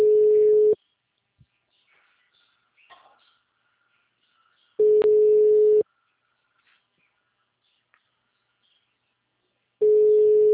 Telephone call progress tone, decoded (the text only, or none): ringback tone